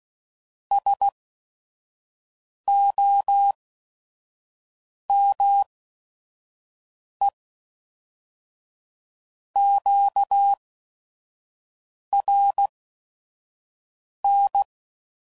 Morse code, heard as a keyed tone, about 16 wpm, spaced as for 5 wpm, 800 Hz. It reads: SOME QRN
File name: CW Code Groups @5WPM / QSOs / WD0FBT 1